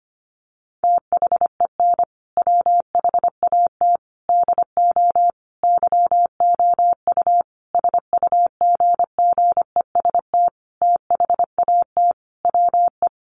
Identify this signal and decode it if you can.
Morse code, heard as a keyed tone, about 25 wpm, 700 Hz. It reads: THEN WHAT DO YOU SUGGEST THAT WE
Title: Sentences from Top 1300 Words 25wpm